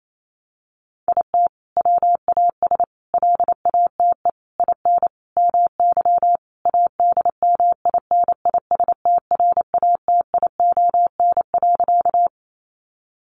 Morse code, heard as a keyed tone, about 28 wpm, 700 Hz. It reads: IT WAS LATE IN MY ADMINISTRATION.
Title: Sentences from Top 1300 Words 28wpm